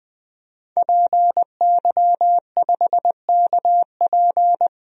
Morse code, heard as a keyed tone, about 20 wpm, 700 Hz. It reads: PY5KP
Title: DXCC-100 Call Signs - 5 Characters - 20wpm